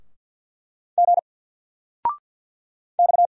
Morse code, heard as a keyed tone, about 50 wpm, 700 Hz. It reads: G X
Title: Single Letter-Number 50wpm